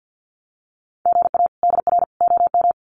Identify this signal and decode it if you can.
Morse code, heard as a keyed tone, about 50 wpm, 700 Hz. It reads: GA DR OM
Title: Common QSO Phrases 50wpm